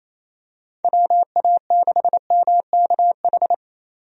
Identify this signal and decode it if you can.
Morse code, heard as a keyed tone, about 28 wpm, 700 Hz. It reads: WA6MKH
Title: USA Call Signs - 6 Characters - 28wpm